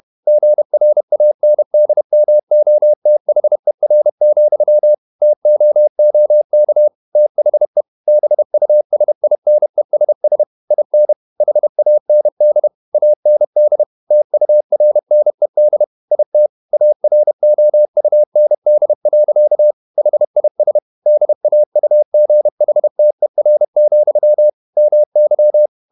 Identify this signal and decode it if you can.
Morse code, heard as a keyed tone, about 31 wpm, 600 Hz. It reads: GRANDMOTHER, TOOK THE BUSINESS IN HAND AND TURNED IT AROUND. HIS DAUGHTER, MY
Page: Never 31wpm | CW med Gnister